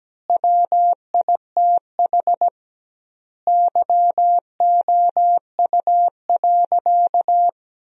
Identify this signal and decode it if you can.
Morse code, heard as a keyed tone, about 17 wpm, 700 Hz. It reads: WITH YOU.
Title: 2x Word-Spacing - Sentences from Top 500 Words - 17wpm